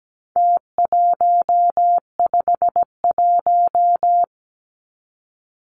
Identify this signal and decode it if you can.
Morse code, heard as a keyed tone, about 17 wpm, 700 Hz. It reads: T151
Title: Numbers - Contest Style - 4-Digit 17wpm